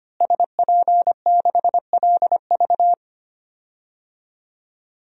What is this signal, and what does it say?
Morse code, heard as a keyed tone, about 25 wpm, 700 Hz. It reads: SP6LV
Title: DXCC-100 Call Signs - 5 Characters - Repeated 2x - 25wpm